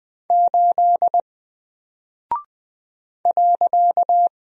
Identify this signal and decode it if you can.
Morse code, heard as a keyed tone, about 20 wpm, 700 Hz. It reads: 8 .
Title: Single Character (Letter:Number:Punctuation:Prosign) 20wpm